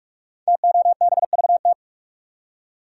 Morse code, heard as a keyed tone, about 44 wpm, 700 Hz. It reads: TOBVT
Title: Code Groups - 5 Letters - 45wpm